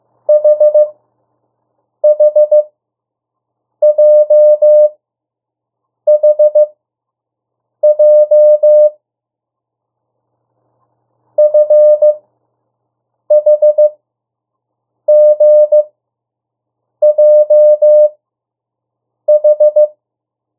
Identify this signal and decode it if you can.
Morse code, heard as a keyed tone, about 15 wpm, spaced as for 6 wpm, 600 Hz. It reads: HHJHJ FHGJH
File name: urok 1